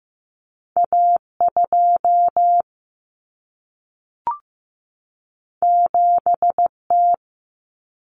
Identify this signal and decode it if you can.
Morse code, heard as a keyed tone, about 15 wpm, 700 Hz. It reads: A2 7T
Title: Common Cut Numbers - 2-Digit 15wpm